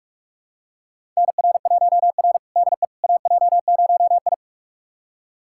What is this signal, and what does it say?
Morse code, heard as a keyed tone, about 46 wpm, 700 Hz. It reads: NR1R DE AJ0I
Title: USA Callsign DE Callsign - 4 Characters - 45wpm